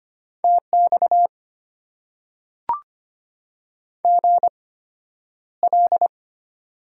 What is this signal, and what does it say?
Morse code, heard as a keyed tone, about 25 wpm, spaced as for 11 wpm, 700 Hz. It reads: TX GL
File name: Top 90 QSO Elements - 8x Character Spacing - 25wpm